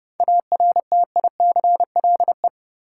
Morse code, heard as a keyed tone, about 30 wpm, 700 Hz. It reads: ARTICLE
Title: Sentences from Top 1000 Words - Review Part 5 - New Word Drill - 30wpm